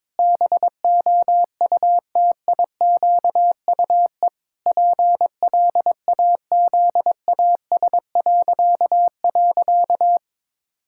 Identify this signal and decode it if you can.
Morse code, heard as a keyed tone, about 22 wpm, 700 Hz. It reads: BOUTIQUE PLAZAS..
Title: Pangram Challenge v2 - 22wpm